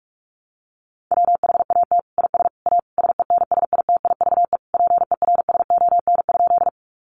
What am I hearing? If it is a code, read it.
Morse code, heard as a keyed tone, about 44 wpm, 700 Hz. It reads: WHAT IS A SENSITIVE PERSON?
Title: Sentences from Top 1400 Words 45wpm